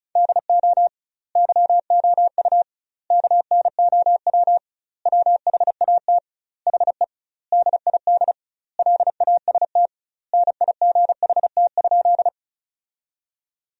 Morse code, heard as a keyed tone, about 36 wpm, 700 Hz. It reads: DO YOU KNOW WHAT HE DID LAST NIGHT?
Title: 2x Word-Spacing - Sentences from Top 300 Words - 35wpm